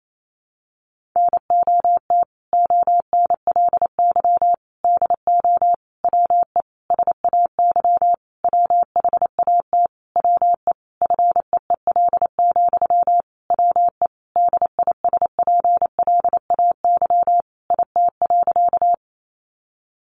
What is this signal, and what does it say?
Morse code, heard as a keyed tone, about 28 wpm, 700 Hz. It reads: NOT ONLY DO WE SAY WHAT WE FEEL, WE DISPLAY IT.
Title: Sentences from Top 1200 Words 28wpm